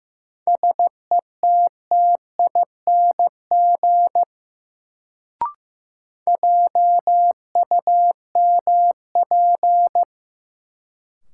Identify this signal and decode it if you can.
Morse code, heard as a keyed tone, about 15 wpm, 700 Hz. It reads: SETTING JUMP
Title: Sentences from Top 1300 Words - New Word Drill Alt - 15wpm